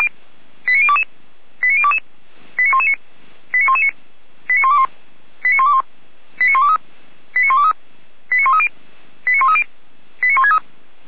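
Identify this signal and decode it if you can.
Five-tone selective call (ZVEI), heard as quick runs of five tones, tones of about 70 ms. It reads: 89022 89022 89119 89119 89121 89121 89123 89123 89130 89130 89163
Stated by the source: Five-tone ZVEI selective call received by SDR.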